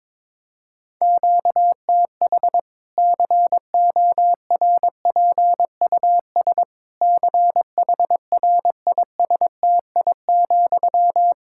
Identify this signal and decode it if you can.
Morse code, heard as a keyed tone, about 22 wpm, 700 Hz. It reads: QTH CORPUS CHRISTI,
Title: K1USN SST QSO Practice 22wpm